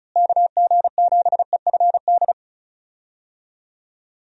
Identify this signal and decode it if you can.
Morse code, heard as a keyed tone, about 34 wpm, 700 Hz. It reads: KG7EFD
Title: USA Call Signs - 6 Characters - Repeated 3x - 35wpm